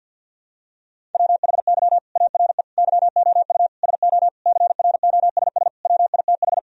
Morse code, heard as a keyed tone, about 51 wpm, 700 Hz. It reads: WHY ARE YOU SO CROSS WITH
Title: Sentences from Top 1000 Words - Review Part 1 - 50wpm